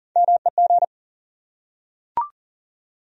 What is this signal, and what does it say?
Morse code, heard as a keyed tone, about 40 wpm, 700 Hz. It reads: MEG
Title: USA Common Names - Max 3 Letters - 40wpm